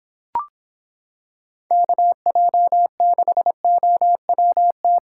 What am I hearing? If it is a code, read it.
Morse code, heard as a keyed tone, about 25 wpm, 700 Hz. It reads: KJ6OWT